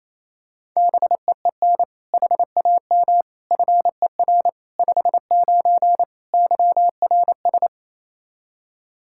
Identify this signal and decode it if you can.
Morse code, heard as a keyed tone, about 28 wpm, 700 Hz. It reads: BEEN HAM FER 59 YRS
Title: Practice QSO - Exchange 3 - More Info - 28wpm